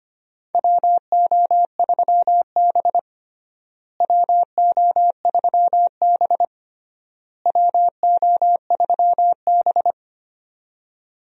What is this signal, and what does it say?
Morse code, heard as a keyed tone, about 25 wpm, 700 Hz. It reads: WO3B WO3B WO3B
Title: USA Call Signs - 4 Characters - Repeated 3x - 25wpm